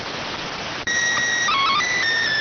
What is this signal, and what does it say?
SSTV (slow-scan television), PD 290.